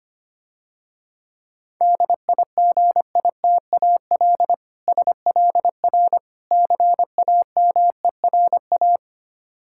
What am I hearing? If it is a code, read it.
Morse code, heard as a keyed tone, about 26 wpm, 700 Hz. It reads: DIGITAL SLR CAMERA
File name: practice / 25wpm / Difficult Words - Speed-Racing with Hints - 25wpm